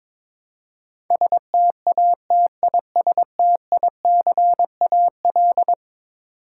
Morse code, heard as a keyed tone, about 22 wpm, 700 Hz. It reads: STATISTICAL
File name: practice / 22wpm / Eleven Letter Words - 22wpm